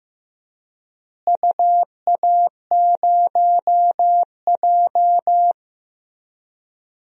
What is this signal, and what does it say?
Morse code, heard as a keyed tone, about 15 wpm, 700 Hz. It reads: UA0J
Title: DXCC-100 Call Signs - 4 Characters - Repeated 3x - 15wpm